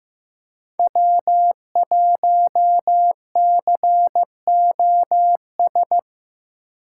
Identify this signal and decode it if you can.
Morse code, heard as a keyed tone, about 15 wpm, 700 Hz. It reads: W1COS
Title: USA Call Signs - 5 Characters - 8x Character Spacing - 15wpm